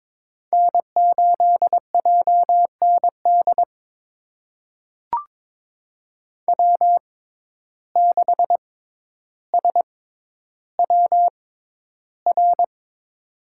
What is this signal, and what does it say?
Morse code, heard as a keyed tone, about 22 wpm, spaced as for 11 wpm, 700 Hz. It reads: N8JND W6SWR
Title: USA Call Signs - 5 Characters - 6x Character Spacing - 22wpm